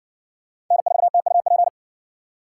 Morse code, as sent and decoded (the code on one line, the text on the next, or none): -. ....- - ..- .--.
N4TUP